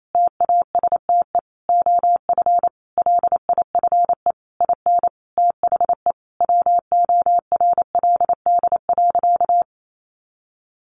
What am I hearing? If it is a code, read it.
Morse code, heard as a keyed tone, about 28 wpm, 700 Hz. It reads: TASTE OF LIFE IN THE WORLD.
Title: Sentences from Top 1100 Words 28wpm